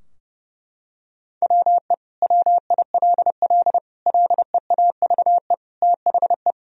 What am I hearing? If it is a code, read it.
Morse code, heard as a keyed tone, about 32 wpm, 700 Hz. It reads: WE WILL LEAVE THE
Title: Sentences from Top 1000 Words - Review Part 2 - 30wpm